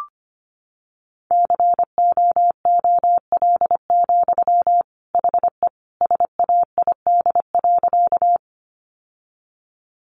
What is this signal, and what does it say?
Morse code, heard as a keyed tone, about 25 wpm, 700 Hz. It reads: COOL, HE SAID.
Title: Sentences from Top 1200 Words 25wpm